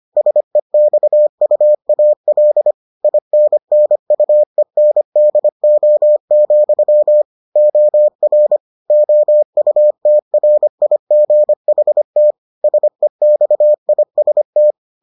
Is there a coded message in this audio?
Yes: Morse code from 0.2 to 14.7 s, heard as a keyed tone, about 25 wpm, 600 Hz. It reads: SEXUAL INNUENDO, OR OUTRIGHT SEXIST